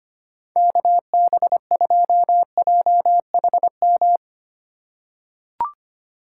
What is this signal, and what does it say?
Morse code, heard as a keyed tone, about 25 wpm, 700 Hz. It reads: KB2JHM
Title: USA Call Signs - 4x Character Spacing - 25wpm